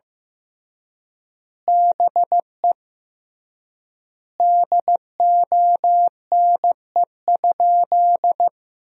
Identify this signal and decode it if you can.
Morse code, heard as a keyed tone, about 15 wpm, 700 Hz. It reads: BE DONE?
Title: 3x Word-Spacing - Sentences from Top 500 Words - 15wpm